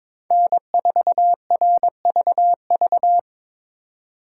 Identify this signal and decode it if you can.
Morse code, heard as a keyed tone, about 26 wpm, 700 Hz. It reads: N4RVV
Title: Speed Racing - USA 1x3 Call Signs 22wpm